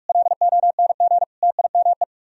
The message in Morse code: LONG TIME

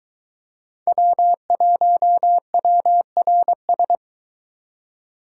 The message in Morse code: W1WRS

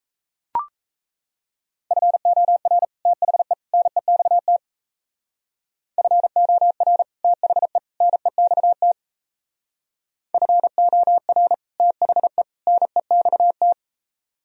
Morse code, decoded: FOR THE NEXT FOR THE NEXT FOR THE NEXT

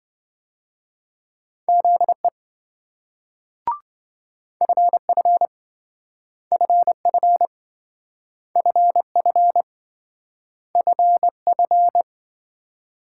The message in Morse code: ZE FF FF FF FF

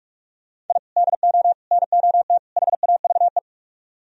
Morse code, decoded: I DO NOT HAVE

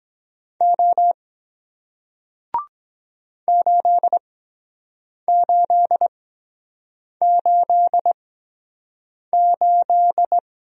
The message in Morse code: O 8888

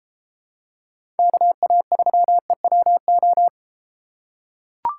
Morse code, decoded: KA3EWO

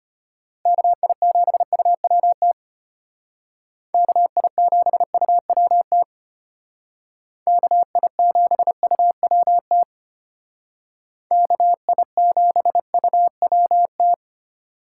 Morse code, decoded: KI7UWT KI7UWT KI7UWT KI7UWT

